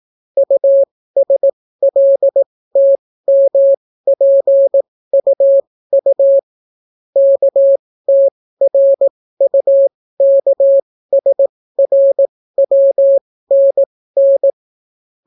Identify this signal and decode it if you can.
Morse code, heard as a keyed tone, about 18 wpm, spaced as for 14 wpm, 550 Hz. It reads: USLTMPUU KTRUKSRWNN